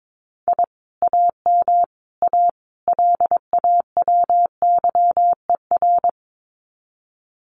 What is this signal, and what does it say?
Morse code, heard as a keyed tone, about 22 wpm, 700 Hz. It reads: I AM A LAWYER
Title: Common QSO Phrases Encore 22wpm